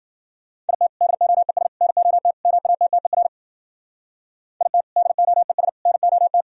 Morse code, heard as a keyed tone, about 50 wpm, 700 Hz. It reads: IT DOES NOT MATTER IT DOES NOT